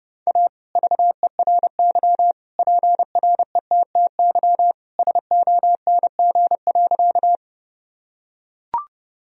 Morse code, heard as a keyed tone, about 30 wpm, 700 Hz. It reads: A VERY PRETTY SONG.
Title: Sentences from Top 1000 Words - Review Part 2 - 30wpm